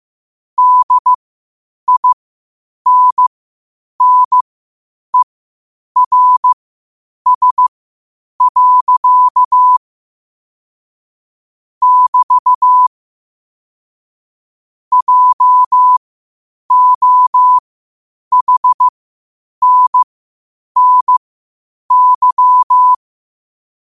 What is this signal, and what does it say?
Morse code, heard as a keyed tone, about 15 wpm, spaced as for 8 wpm, 1000 Hz. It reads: DINNERS. <BT> JOHNNY
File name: qotdfiles / 8wpm